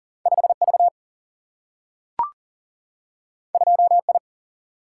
Morse code, heard as a keyed tone, about 39 wpm, 700 Hz. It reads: 5V 2I
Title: Single-Letter Single-Number 40wpm